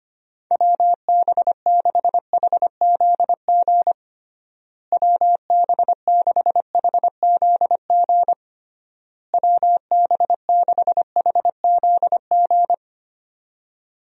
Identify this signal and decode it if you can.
Morse code, heard as a keyed tone, about 25 wpm, 700 Hz. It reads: WB6HZG WB6HZG WB6HZG
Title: USA Call Signs - 6 Characters - Repeated 3x - 25wpm